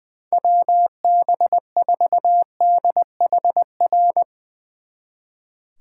Morse code, heard as a keyed tone, about 22 wpm, 700 Hz. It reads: WB4DHR
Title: Speed Racing - USA 2x3 Call Signs 20wpm